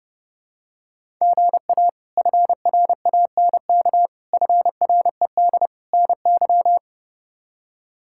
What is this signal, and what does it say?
Morse code, heard as a keyed tone, about 30 wpm, 700 Hz. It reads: GA FRANK FRED NY
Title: K1USN SST QSO Practice - 30wpm